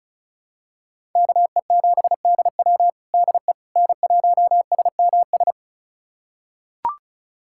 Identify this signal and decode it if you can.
Morse code, heard as a keyed tone, about 36 wpm, 700 Hz. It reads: KE7DW DE N1SMS
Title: USA Callsign DE Callsign - 5 Characters - 35wpm